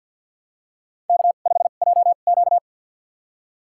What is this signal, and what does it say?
Morse code, heard as a keyed tone, about 48 wpm, spaced as for 35 wpm, 700 Hz. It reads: K5JQ
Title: USA Call Signs - 4 Characters - 2x Character Spacing - 50wpm